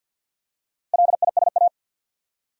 Morse code, heard as a keyed tone, about 52 wpm, 700 Hz. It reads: LISA
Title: USA Common Names - Max 4 Letters - 50wpm